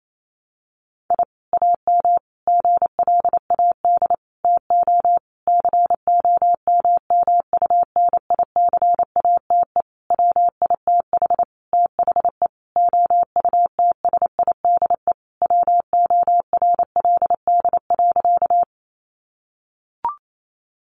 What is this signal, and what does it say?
Morse code, heard as a keyed tone, about 28 wpm, 700 Hz. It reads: I AM GLAD TO COMMUNICATE WITH THE OUTSIDE WORLD.
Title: Sentences from Top 1300 Words 28wpm